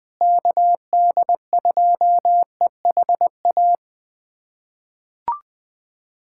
Morse code, heard as a keyed tone, about 20 wpm, 700 Hz. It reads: KD2EHA
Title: USA Callsign DE Callsign - 6 Characters - 20wpm